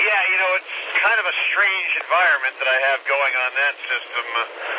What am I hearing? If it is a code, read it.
Emergency Alert System (SAME) tones, heard as harsh buzzing bursts, with speech. It is heard from 0.8 to 4.0 s.